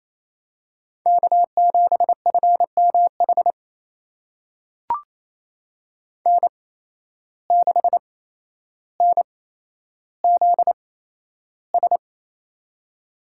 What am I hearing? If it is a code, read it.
Morse code, heard as a keyed tone, about 28 wpm, spaced as for 12 wpm, 700 Hz. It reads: K7FMH N6NZS
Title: USA Call Signs - 5 Characters - 8x Character Spacing - 28wpm